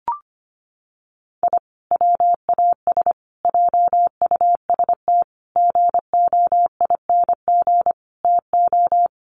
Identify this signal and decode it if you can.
Morse code, heard as a keyed tone, about 25 wpm, 700 Hz. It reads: I WAS JUST GOING TO
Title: Sets of 5 Words Encore 25wpm